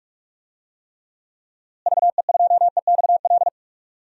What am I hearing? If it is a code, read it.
Morse code, heard as a keyed tone, about 46 wpm, 700 Hz. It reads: VE2EXL
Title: DXCC-100 Call Signs - 6 Characters - Repeated 3x - 45wpm